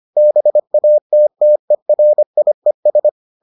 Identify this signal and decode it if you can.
Morse code, heard as a keyed tone, about 25 wpm, 600 Hz. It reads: BATTERIES